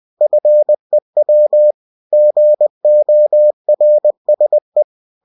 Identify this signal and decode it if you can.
Morse code, heard as a keyed tone, about 20 wpm, 600 Hz. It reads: FEW GORSE
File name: cw-books / winnie-the-pooh / winnie-20wpm-page-0036